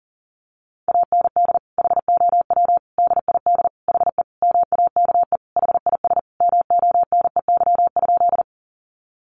Morse code, heard as a keyed tone, about 40 wpm, 700 Hz. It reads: AND HOW DID HE MAKE HIS MONEY?
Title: Sentences from Top 400 Words 40wpm